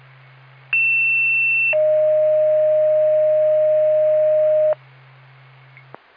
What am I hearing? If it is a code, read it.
Two-tone page, heard as a one-second tone, then a three-second tone, about 2700 then 620 Hz. It is heard from 0.7 to 4.7 s.